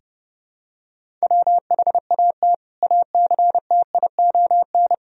Morse code, heard as a keyed tone, about 30 wpm, 700 Hz. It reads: WHAT ACTION
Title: Sentences from Top 1000 Words - Review Part 4 - 30wpm